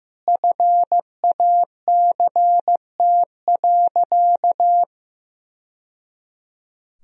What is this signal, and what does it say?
Morse code, heard as a keyed tone, about 15 wpm, 700 Hz. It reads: FACT.
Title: Sentences from Top 1000 Words - Review Part 4 - 15wpm